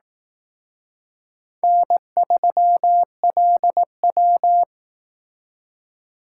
Morse code, heard as a keyed tone, about 18 wpm, 700 Hz. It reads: N3LW
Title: Speed Racing - USA 1x2 Call Signs 15wpm